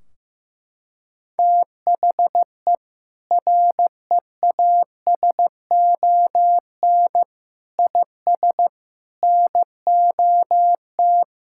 Morse code, heard as a keyed tone, about 15 wpm, 700 Hz. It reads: THE REASON IS NOT
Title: Sentences from Top 1600 Words 15wpm